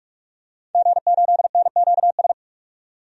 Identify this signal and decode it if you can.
Morse code, heard as a keyed tone, about 46 wpm, 700 Hz. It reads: G8NQS
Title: DXCC-100 Call Signs - 5 Characters - 45wpm